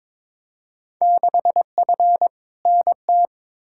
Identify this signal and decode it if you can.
Morse code, heard as a keyed tone, about 22 wpm, 700 Hz. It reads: 6F NT
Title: Field Day Exchanges Encore - 22wpm